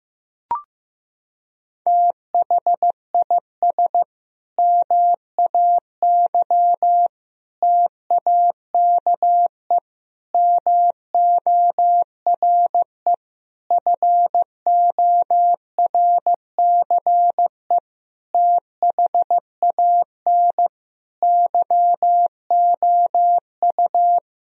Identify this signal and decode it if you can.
Morse code, heard as a keyed tone, about 15 wpm, 700 Hz. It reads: THIS MAY TAKE MORE FORCE THAN YOU
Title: Sentences from Top 1000 Words - Review Part 3 - 15wpm